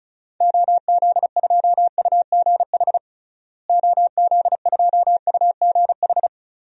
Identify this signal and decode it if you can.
Morse code, heard as a keyed tone, about 36 wpm, 700 Hz. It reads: OZ2UGH OZ2UGH
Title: DXCC-100 Call Signs - 6 Characters - Repeated 3x - 35wpm